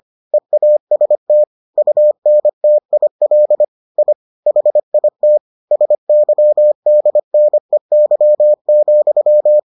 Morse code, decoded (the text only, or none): EAST UNTIL I HIT SYDNEY,